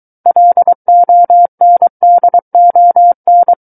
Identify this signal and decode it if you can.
Morse code, heard as a keyed tone, about 23 wpm, 700 Hz. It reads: LONDON
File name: Book One / 14-Chapter 14 - 23 WPM